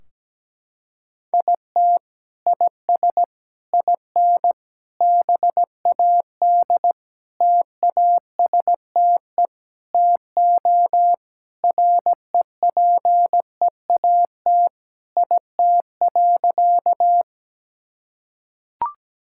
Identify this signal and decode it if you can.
Morse code, heard as a keyed tone, about 17 wpm, 700 Hz. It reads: IT IS IN BAD TASTE TO REPEAT IT.
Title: Sentences from Top 1600 Words 17wpm